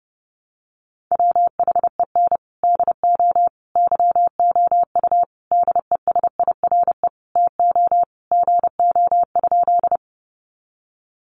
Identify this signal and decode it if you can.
Morse code, heard as a keyed tone, about 30 wpm, 700 Hz. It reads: WHEN DO YOU DESIRE TO GO?
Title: Sentences from Top 1100 Words 30wpm